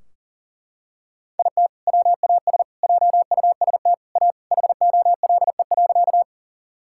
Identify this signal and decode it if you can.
Morse code, heard as a keyed tone, about 40 wpm, 700 Hz. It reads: IT WAS JUST A HOLE.
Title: Sentences from Top 1600 Words 40wpm